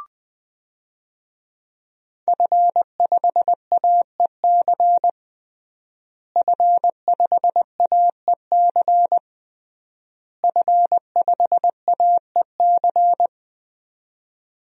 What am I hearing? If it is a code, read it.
Morse code, heard as a keyed tone, about 21 wpm, 700 Hz. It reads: F5AEC F5AEC F5AEC
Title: DXCC-100 Call Signs - 5 Characters - Repeated 3x - 20wpm